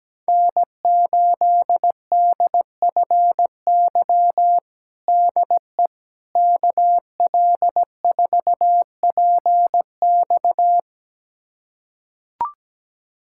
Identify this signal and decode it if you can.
Morse code, heard as a keyed tone, about 17 wpm, 700 Hz. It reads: N8DFY DE KL4PX
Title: USA Callsign DE Callsign - 5 Characters - 17wpm